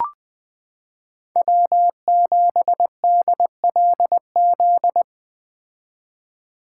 Morse code, heard as a keyed tone, about 20 wpm, 700 Hz. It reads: W7DLZ
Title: 2023 POTA Activator Call Signs - 20wpm